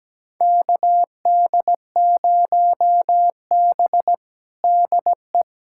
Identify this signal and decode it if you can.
Morse code, heard as a keyed tone, about 17 wpm, 700 Hz. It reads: KD0B DE
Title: USA Callsign DE Callsign - 4 Characters - 17wpm